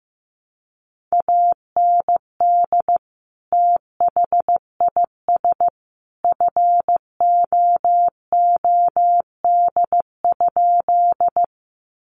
Morse code, heard as a keyed tone, about 15 wpm, 700 Hz. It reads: AND THIS FOOD?
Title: Sentences from Top 300 Words 15wpm